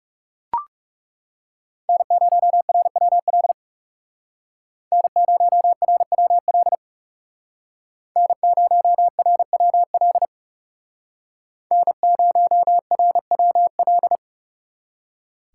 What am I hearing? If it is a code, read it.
Morse code, heard as a keyed tone, about 37 wpm, 700 Hz. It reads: N0RWL N0RWL N0RWL N0RWL
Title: Speed Racing - USA 1x3 Call Signs 30wpm